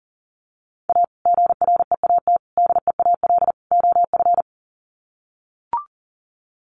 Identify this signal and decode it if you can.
Morse code, heard as a keyed tone, about 40 wpm, 700 Hz. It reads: A GREAT DEAL OF
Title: Sets of 4 Words 40wpm